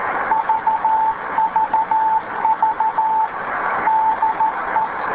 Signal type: Morse code